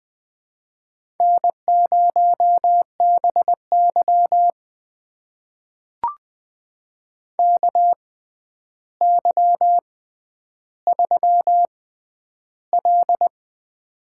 Morse code, heard as a keyed tone, about 21 wpm, spaced as for 10 wpm, 700 Hz. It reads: N0BY KY3L
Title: USA Call Signs - 4 Characters - 6x Character Spacing - 20wpm